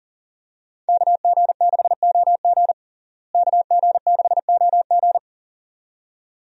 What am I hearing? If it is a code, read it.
Morse code, heard as a keyed tone, about 40 wpm, 700 Hz. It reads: KG6OG KG6OG
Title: USA Call Signs - 5 Characters - Repeated 2x - 40wpm